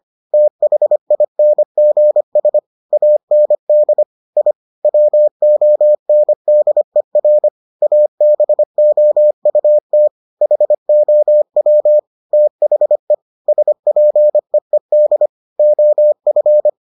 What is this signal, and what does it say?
Morse code, heard as a keyed tone, about 25 wpm, 600 Hz. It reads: THINGS AND I WONDER ABOUT HOW THE SPEED OF